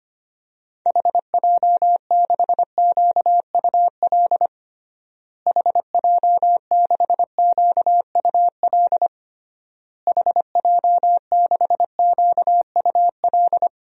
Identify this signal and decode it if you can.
Morse code, heard as a keyed tone, about 25 wpm, 700 Hz. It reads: HJ6QUL HJ6QUL HJ6QUL
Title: DXCC-100 Call Signs - 6 Characters - Repeated 3x - 25wpm